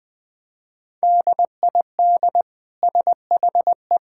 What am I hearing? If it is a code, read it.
Morse code, heard as a keyed tone, about 20 wpm, 700 Hz. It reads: DID SHE
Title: Sentences from Top 1600 Words 20wpm